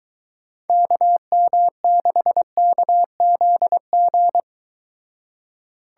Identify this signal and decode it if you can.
Morse code, heard as a keyed tone, about 22 wpm, 700 Hz. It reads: KM6KZG